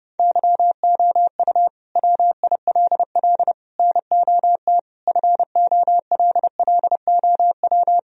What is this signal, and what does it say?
Morse code, heard as a keyed tone, about 30 wpm, 700 Hz. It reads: YOU WILL NOT FOLLOW
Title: Sentences from Top 1000 Words - Review Part 1 - 30wpm